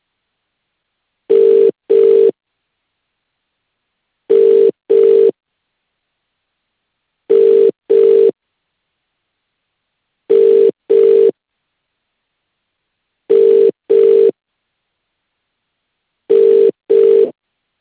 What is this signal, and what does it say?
Ringback tone (UK), 400 + 450 Hz.